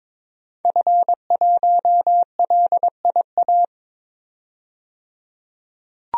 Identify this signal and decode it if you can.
Morse code, heard as a keyed tone, about 22 wpm, 700 Hz. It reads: F1LIA
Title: DXCC-100 Call Signs - 5 Characters - 22wpm